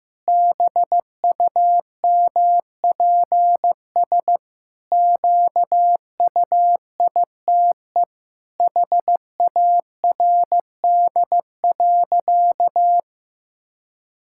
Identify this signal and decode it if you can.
Morse code, heard as a keyed tone, about 15 wpm, 700 Hz. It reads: BUMPS QUITE HARD.
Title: Pangram Challenge v2 - 15wpm